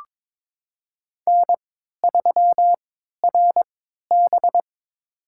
Morse code, heard as a keyed tone, about 22 wpm, spaced as for 13 wpm, 700 Hz. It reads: N3RB
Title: USA Call Signs - 4 Characters - 3x Character Spacing - 22wpm